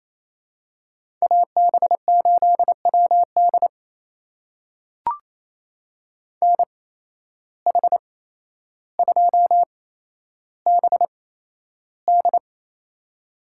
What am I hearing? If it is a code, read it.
Morse code, heard as a keyed tone, about 28 wpm, spaced as for 12 wpm, 700 Hz. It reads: AB8WD NH2BD
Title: USA Call Signs - 5 Characters - 8x Character Spacing - 28wpm